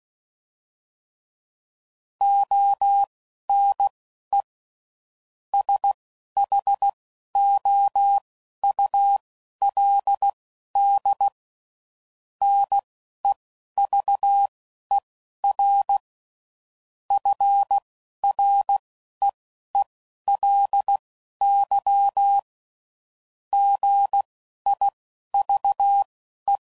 Morse code, heard as a keyed tone, about 16 wpm, spaced as for 11 wpm, 800 Hz. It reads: ONE SHOULD NEVER FREELY GIVE